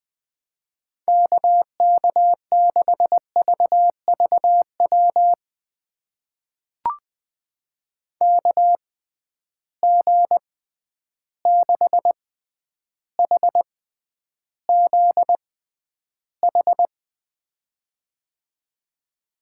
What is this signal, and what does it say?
Morse code, heard as a keyed tone, about 20 wpm, spaced as for 10 wpm, 700 Hz. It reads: KK6VVW KG6HZH
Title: USA Call Signs - 6 Characters - 6x Character Spacing - 20wpm